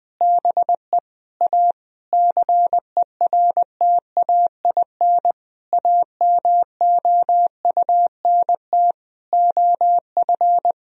Morse code, heard as a keyed tone, about 20 wpm, 700 Hz. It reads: BE A CERTAIN AMOUNT OF
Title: Sentences from Top 1000 Words - Review Part 4 - 20wpm